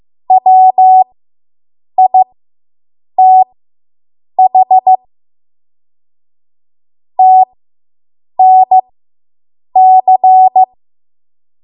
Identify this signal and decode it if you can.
Morse code, heard as a keyed tone, about 15 wpm, spaced as for 7 wpm, 750 Hz. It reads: WITH TNC